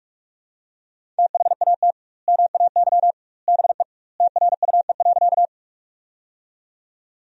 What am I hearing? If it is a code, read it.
Morse code, heard as a keyed tone, about 46 wpm, 700 Hz. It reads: THAT MAY BE TRUE.
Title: 2x Word-Spacing - Sentences from Top 400 Words - 45wpm